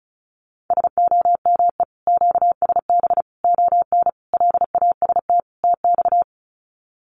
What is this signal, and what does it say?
Morse code, heard as a keyed tone, about 34 wpm, 700 Hz. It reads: SOME QSB ON LAST TX
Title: Common QSO Phrases 35wpm